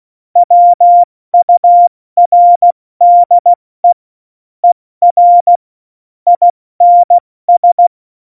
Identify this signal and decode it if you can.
Morse code, heard as a keyed tone, about 16 wpm, spaced as for 14 wpm, 700 Hz. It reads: WURDE ER INS
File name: Morsekurs / Witze / Witz 26. Januar 2021, 70 F80